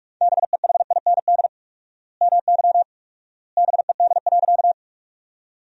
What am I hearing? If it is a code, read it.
Morse code, heard as a keyed tone, about 42 wpm, 700 Hz. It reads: BEHIND MY BED.
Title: 4x Word-Spacing - Sentences from Top 500 Words - 45wpm